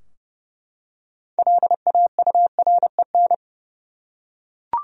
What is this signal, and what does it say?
Morse code, heard as a keyed tone, about 31 wpm, 700 Hz. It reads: LAUREN
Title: USA Common Names - 30wpm